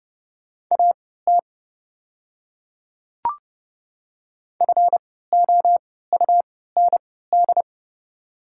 Morse code, heard as a keyed tone, about 30 wpm, spaced as for 17 wpm, 690 Hz. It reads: AT FOUND
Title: 3x Character Spacing - Top 300 Words - 30wpm